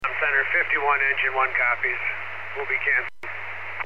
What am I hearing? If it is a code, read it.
Emergency Alert System (SAME) tones, heard as harsh buzzing bursts, with speech. It is heard from 0.1 to 3.1 s.